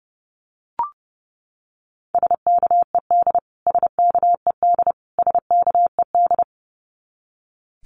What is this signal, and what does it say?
Morse code, heard as a keyed tone, about 30 wpm, 700 Hz. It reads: SKED SKED SKED
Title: Abbreviations Repeated 3x - 30wpm